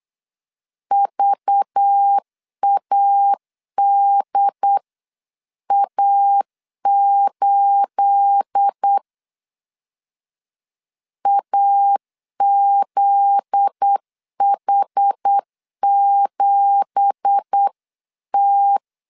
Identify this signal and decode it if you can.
Morse code, heard as a keyed tone, about 8 wpm, 800 Hz. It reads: VADA8 AZH7T
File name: CW / DET / 1 9 1500C S B 8wpm 5624